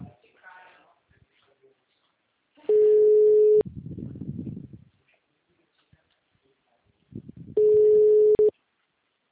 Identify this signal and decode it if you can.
Telephone call progress tone (Europe, 425 Hz), heard as a telephone network's tone, about 420 Hz, with speech. Ringback tone.